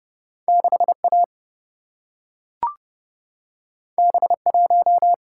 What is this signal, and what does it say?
Morse code, heard as a keyed tone, about 30 wpm, 700 Hz. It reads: 6A B1
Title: Single-Letter Single-Number 30wpm